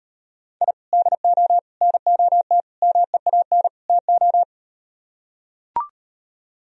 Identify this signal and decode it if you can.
Morse code, heard as a keyed tone, about 38 wpm, 700 Hz. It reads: I DO NOT MEAN TO